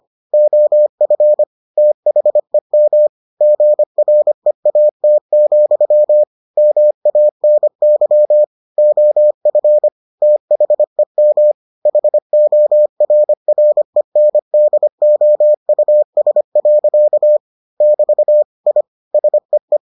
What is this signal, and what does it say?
Morse code, heard as a keyed tone, about 25 wpm, 600 Hz. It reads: OF THEM GREAT, MANY OF THEM HORRENDOUS. <BT> I SEE